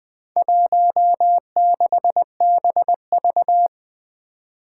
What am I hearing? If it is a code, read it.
Morse code, heard as a keyed tone, about 20 wpm, 700 Hz. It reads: 16BV
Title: Four Challenging Letter-Number Combinations 20wpm